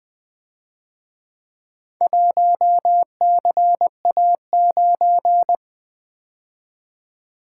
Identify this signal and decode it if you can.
Morse code, heard as a keyed tone, about 24 wpm, 700 Hz. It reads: 1CA9
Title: Speed-Racing - Four Letter-Number Combinations - 20wpm